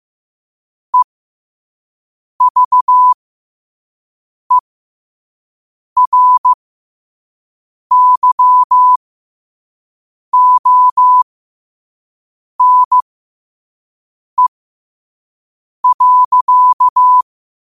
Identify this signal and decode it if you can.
Morse code, heard as a keyed tone, about 15 wpm, spaced as for 5 wpm, 1000 Hz. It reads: EVERYONE.
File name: qotdfiles / 5wpm